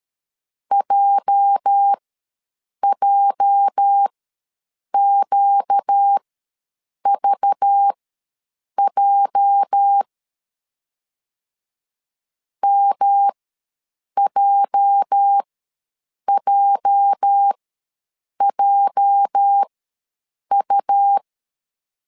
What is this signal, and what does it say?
Morse code, heard as a keyed tone, about 13 wpm, spaced as for 7 wpm, 800 Hz. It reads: JJQVJ MJJJU
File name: LEZIONI / 8 YJ45 / 8 YJ ws 12wpm 3342